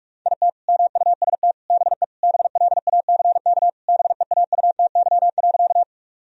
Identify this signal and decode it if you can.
Morse code, heard as a keyed tone, about 46 wpm, 700 Hz. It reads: IT MUST BE BLACK BEAUTY.
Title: Sentences from Top 1000 Words - Review Part 2 - 45wpm